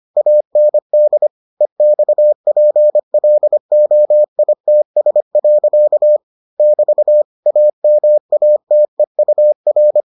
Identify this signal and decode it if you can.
Morse code, heard as a keyed tone, about 25 wpm, 600 Hz. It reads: AND EXPLOITS. <BT> AMATEUR